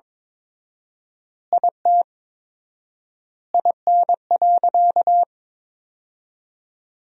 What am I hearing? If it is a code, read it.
Morse code, heard as a keyed tone, about 21 wpm, 700 Hz. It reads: IT IN.
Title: 4x Word-Spacing - Sentences from Top 200 Words - 22wpm